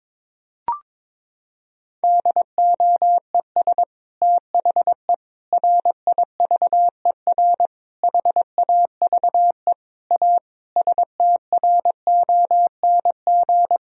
Morse code, heard as a keyed tone, about 22 wpm, 700 Hz. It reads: DOES THE RIVER HAVE A STRONG
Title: Sentences from Top 1000 Words - Review Part 4 - 22wpm